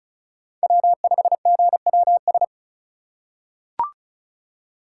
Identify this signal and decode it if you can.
Morse code, heard as a keyed tone, about 36 wpm, 700 Hz. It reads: W5GWS
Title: USA Call Signs - 5 Characters - 2x Character Spacing - 35wpm